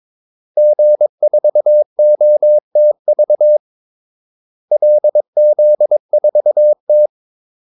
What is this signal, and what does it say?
Morse code, heard as a keyed tone, about 22 wpm, 600 Hz. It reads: G4OTV LZ4T
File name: Callsigns-22wpm